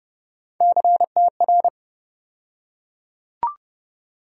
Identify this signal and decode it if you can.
Morse code, heard as a keyed tone, about 30 wpm, 700 Hz. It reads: CTR
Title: USA Call Sign Postfixes - 3 Letters 30wpm